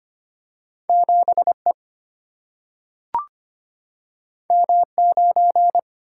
Morse code, heard as a keyed tone, about 25 wpm, 700 Hz. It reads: 7E M9
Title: Single-Letter Single-Number 25wpm